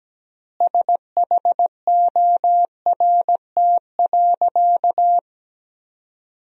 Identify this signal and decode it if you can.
Morse code, heard as a keyed tone, about 17 wpm, 700 Hz. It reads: SHORT.
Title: 3x Word-Spacing - Sentences from Top 500 Words - 17wpm